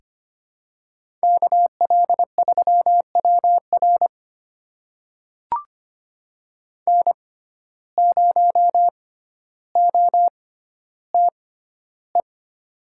Morse code, heard as a keyed tone, about 25 wpm, spaced as for 13 wpm, 700 Hz. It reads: KL3WR N0OTE
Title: USA Call Signs - 5 Characters - 6x Character Spacing - 25wpm